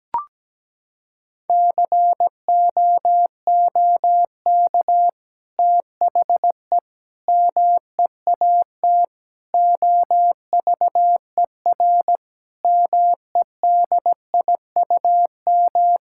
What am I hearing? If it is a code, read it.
Morse code, heard as a keyed tone, about 17 wpm, 700 Hz. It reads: COOK THE MEAT OVER MEDIUM
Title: Sentences from Top 1000 Words - Review Part 6 - 17wpm